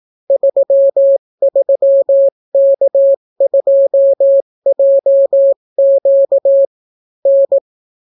Morse code, heard as a keyed tone, about 18 wpm, 550 Hz. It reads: 33K2JQ N